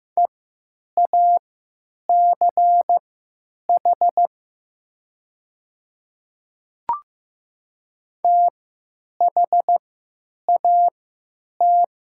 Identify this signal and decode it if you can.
Morse code, heard as a keyed tone, about 15 wpm, spaced as for 8 wpm, 700 Hz. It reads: EACH THAT
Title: 3x Character Spacing - Top 100 Words - 15wpm